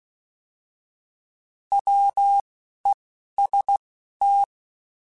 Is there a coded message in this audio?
Yes: Morse code from 1.7 to 4.4 s, heard as a keyed tone, about 16 wpm, spaced as for 11 wpm, 800 Hz. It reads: WEST